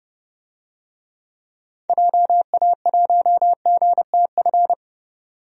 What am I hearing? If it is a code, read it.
Morse code, heard as a keyed tone, about 30 wpm, 700 Hz. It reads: JA1GTF
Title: DXCC-100 Call Signs - 6 Characters - 30wpm